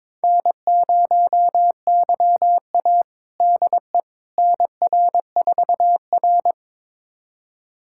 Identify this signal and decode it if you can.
Morse code, heard as a keyed tone, about 22 wpm, 700 Hz. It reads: N0YA DE NR4R
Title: USA Callsign DE Callsign - 4 Characters - 22wpm